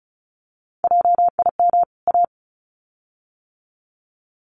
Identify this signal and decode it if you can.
Morse code, heard as a keyed tone, about 34 wpm, 700 Hz. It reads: JIM A
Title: CWT Exchange 35wpm